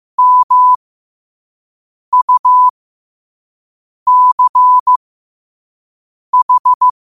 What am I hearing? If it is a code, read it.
Morse code, heard as a keyed tone, about 15 wpm, spaced as for 5 wpm, 1000 Hz. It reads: MUCH
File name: qotdfiles / 5wpm